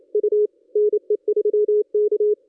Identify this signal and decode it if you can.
Morse code, heard as a keyed tone, about 28 wpm, 440 Hz. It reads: U NE3K
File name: contests / ss-cw-k5tr-op-n5rz -1400